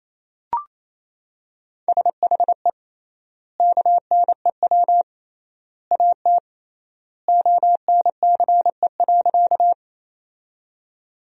Morse code, heard as a keyed tone, about 28 wpm, 700 Hz. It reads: SHE KNEW AT ONCE.
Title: 3x Word-Spacing - Sentences from Top 400 Words - 28wpm